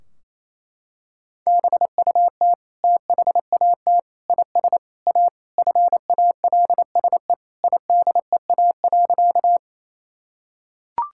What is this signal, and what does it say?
Morse code, heard as a keyed tone, about 27 wpm, 700 Hz. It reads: BUT THAT IS A FALSE IDEA.
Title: Sentences from Top 1600 Words 28wpm